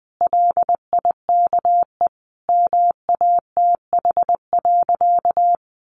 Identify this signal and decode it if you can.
Morse code, heard as a keyed tone, about 20 wpm, 700 Hz. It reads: LIKE MATH.
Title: Easy Phrases 20wpm